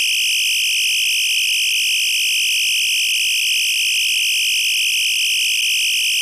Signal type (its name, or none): two-tone teleprinter signal